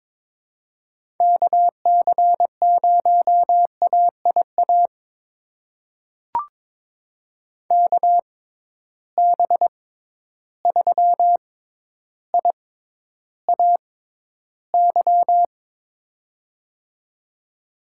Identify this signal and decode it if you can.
Morse code, heard as a keyed tone, about 22 wpm, spaced as for 11 wpm, 700 Hz. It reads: KC0AIA KB3IAY
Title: USA Call Signs - 6 Characters - 6x Character Spacing - 22wpm